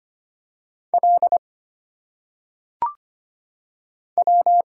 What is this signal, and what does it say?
Morse code, heard as a keyed tone, about 25 wpm, 700 Hz. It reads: L W
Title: Single Letters - Phonetic Pronunciation 25wpm